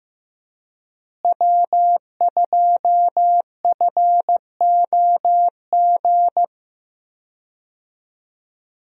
Morse code, heard as a keyed tone, about 15 wpm, 700 Hz. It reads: W2FOG
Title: USA Call Signs - 5 Characters - Repeated 3x - 15wpm